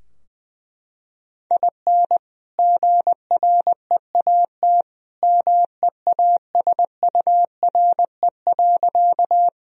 Morse code, heard as a keyed tone, about 20 wpm, 700 Hz. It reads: IN GREAT MEASURE.
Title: Sentences from Top 1000 Words - Review Part 2 - 20wpm